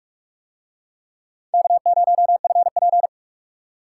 Morse code, heard as a keyed tone, about 44 wpm, 700 Hz. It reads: K0FP